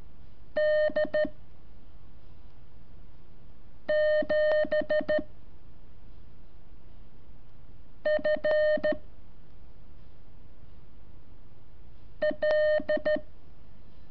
Morse code, heard as a keyed tone, about 13 wpm, 610 Hz. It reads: D 7 F L
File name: curso / ejercicio 23